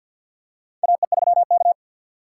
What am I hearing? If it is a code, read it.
Morse code, heard as a keyed tone, about 47 wpm, 700 Hz. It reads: AE3K
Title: USA Call Signs - 4 Characters - 6x Character Spacing - 50wpm